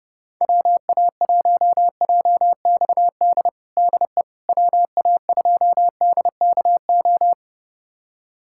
Morse code, heard as a keyed tone, about 30 wpm, 700 Hz. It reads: WA1JXD DE WA2DKO
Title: USA Callsign DE Callsign - 6 Characters - 30wpm